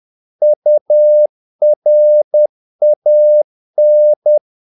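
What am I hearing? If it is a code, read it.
Morse code, heard as a keyed tone, about 10 wpm, 600 Hz. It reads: URAN
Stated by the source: Morse practice of short Danish words at 10 wpm, from CW med Gnister.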